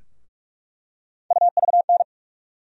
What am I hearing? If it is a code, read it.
Morse code, heard as a keyed tone, about 43 wpm, 700 Hz. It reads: UVN
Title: USA Call Sign Postfixes - 3 Letters 45wpm